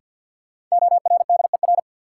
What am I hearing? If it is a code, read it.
Morse code, heard as a keyed tone, about 48 wpm, 700 Hz. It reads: ORDER
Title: Sentences from Top 400 Words - New Word Drill - 50wpm